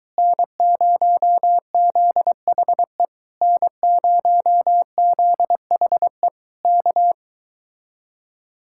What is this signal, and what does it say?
Morse code, heard as a keyed tone, about 23 wpm, 700 Hz. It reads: N0ZHE N0ZHE K